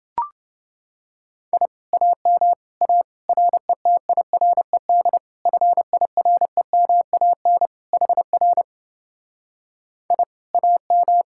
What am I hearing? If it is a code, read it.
Morse code, heard as a keyed tone, about 28 wpm, 700 Hz. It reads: I AM A RETIRED FIREMAN HR I AM